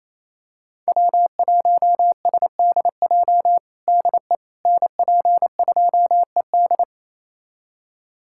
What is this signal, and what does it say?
Morse code, heard as a keyed tone, about 27 wpm, 700 Hz. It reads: W1SDJ DE NP2ED
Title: USA Callsign DE Callsign - 5 Characters - 28wpm